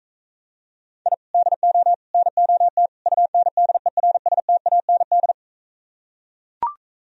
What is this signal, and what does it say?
Morse code, heard as a keyed tone, about 43 wpm, 700 Hz. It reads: I DO NOT UNDERSTAND